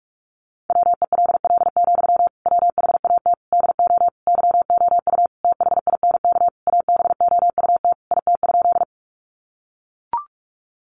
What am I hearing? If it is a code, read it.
Morse code, heard as a keyed tone, about 44 wpm, 700 Hz. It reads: WELL, WHAT DO YOU THINK ABOUT IT?
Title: Sentences from Top 200 Words 45wpm